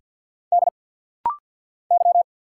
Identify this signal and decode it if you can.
Morse code, heard as a keyed tone, about 50 wpm, 690 Hz. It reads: D Y
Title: Single Letters - Rapid-Fire 50wpm